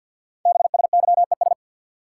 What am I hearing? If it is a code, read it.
Morse code, heard as a keyed tone, about 48 wpm, 700 Hz. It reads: BSYES
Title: Code Groups - 5 Letters - 50wpm